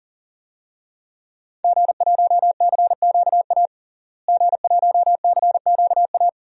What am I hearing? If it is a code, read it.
Morse code, heard as a keyed tone, about 40 wpm, 700 Hz. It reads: G1CQA G1CQA
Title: DXCC-100 Call Signs - 5 Characters - Repeated 3x - 40wpm